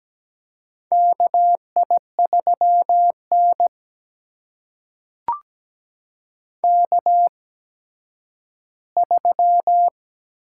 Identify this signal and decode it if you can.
Morse code, heard as a keyed tone, about 17 wpm, spaced as for 10 wpm, 700 Hz. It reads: KI3N K3
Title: USA Call Signs - 4 Characters - 8x Character Spacing - 17wpm